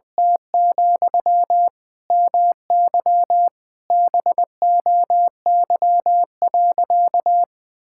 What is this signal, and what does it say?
Morse code, heard as a keyed tone, about 20 wpm, 700 Hz. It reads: T, MY BOY.
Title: Sentences from Top 1000 Words - Review Part 5 - 20wpm